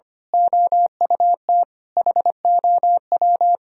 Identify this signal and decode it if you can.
Morse code, heard as a keyed tone, about 28 wpm, 700 Hz. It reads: OUT HOW
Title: Speed-Racing - Sets of 5 Words 25wpm